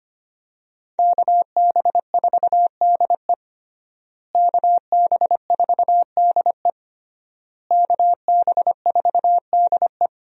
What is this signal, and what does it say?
Morse code, heard as a keyed tone, about 25 wpm, 700 Hz. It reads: KB4DE KB4DE KB4DE
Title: USA Call Signs - 5 Characters - Repeated 3x - 25wpm